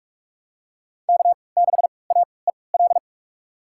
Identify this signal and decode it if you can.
Morse code, heard as a keyed tone, about 46 wpm, spaced as for 25 wpm, 700 Hz. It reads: K6AEL
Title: USA Call Signs - 3x Character Spacing - 45wpm